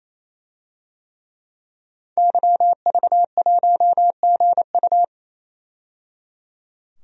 Morse code, heard as a keyed tone, about 28 wpm, 700 Hz. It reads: YV1GU
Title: DXCC-100 Call Signs - 5 Characters - 28wpm